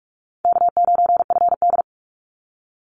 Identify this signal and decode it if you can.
Morse code, heard as a keyed tone, about 46 wpm, 700 Hz. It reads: K9FD
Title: Easy Call Signs - 45wpm